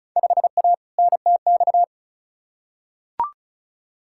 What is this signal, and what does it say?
Morse code, heard as a keyed tone, about 36 wpm, 700 Hz. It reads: 5A NTX
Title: Field Day Exchanges - 35wpm